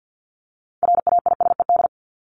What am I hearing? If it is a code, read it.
Morse code, heard as a keyed tone, about 50 wpm, 700 Hz. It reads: RAISED
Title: ED Suffix - Top 100 Words with Base Words 50wpm